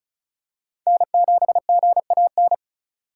Morse code, heard as a keyed tone, about 36 wpm, 700 Hz. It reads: N7GAN
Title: USA Call Signs - 5 Characters - 4x Character Spacing - 35wpm